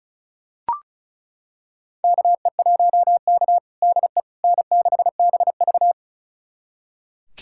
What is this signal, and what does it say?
Morse code, heard as a keyed tone, about 34 wpm, 700 Hz. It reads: KE1K DE N6BV
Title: USA Callsign DE Callsign - 4 Characters - 35wpm